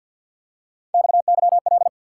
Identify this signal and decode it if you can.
Morse code, heard as a keyed tone, about 50 wpm, 700 Hz. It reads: XYL
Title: Top 90 QSO Elements 50wpm